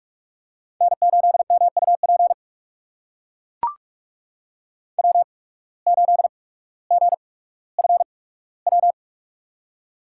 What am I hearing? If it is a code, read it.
Morse code, heard as a keyed tone, about 44 wpm, spaced as for 19 wpm, 690 Hz. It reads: N8MUP W8GFW
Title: USA Call Signs - 5 Characters - 8x Character Spacing - 45wpm